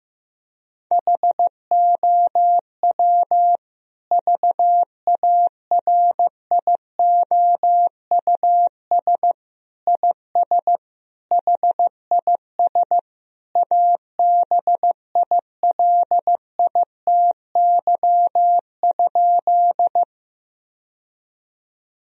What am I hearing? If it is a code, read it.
Morse code, heard as a keyed tone, about 15 wpm, 700 Hz. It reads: HOW VARIOUS IS HIS ABILITY?
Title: Sentences from Top 1000 Words - Review Part 4 - 15wpm